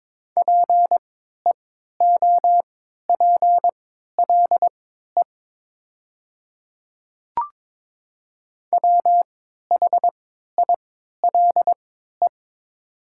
Morse code, heard as a keyed tone, about 22 wpm, spaced as for 13 wpm, 700 Hz. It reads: PEOPLE WHILE
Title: 3x Character Spacing - Top 300 Words - 22wpm